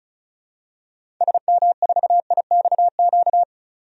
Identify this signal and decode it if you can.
Morse code, heard as a keyed tone, about 35 wpm, 700 Hz. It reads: SM4IXQ
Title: DXCC-100 Call Signs - 6 Characters - 35wpm